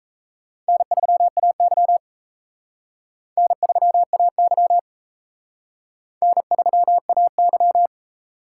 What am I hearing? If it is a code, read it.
Morse code, heard as a keyed tone, about 37 wpm, 700 Hz. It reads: N3AY N3AY N3AY